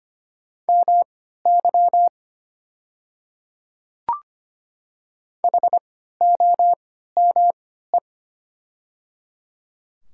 Morse code, heard as a keyed tone, about 25 wpm, spaced as for 14 wpm, 700 Hz. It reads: MY HOME
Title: 3x Character Spacing - Top 300 Words - 25wpm